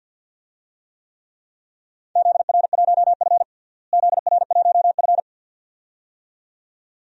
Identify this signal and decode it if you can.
Morse code, heard as a keyed tone, about 46 wpm, 700 Hz. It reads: ZR1F ZR1F
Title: DXCC-100 Call Signs - 4 Characters - Repeated 2x - 50wpm